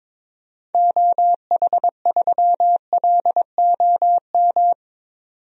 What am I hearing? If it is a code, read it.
Morse code, heard as a keyed tone, about 22 wpm, 700 Hz. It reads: OH3LOM
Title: DXCC-100 Call Signs - 6 Characters - 22wpm